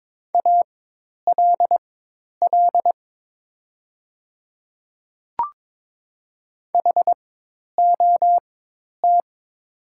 Morse code, heard as a keyed tone, about 22 wpm, spaced as for 10 wpm, 700 Hz. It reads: ALL HOT
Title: 4x Character Spacing - Top 200 Words - 22wpm